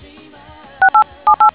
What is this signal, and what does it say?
DTMF, heard as telephone touch tones, tones of about 100 ms. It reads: 6878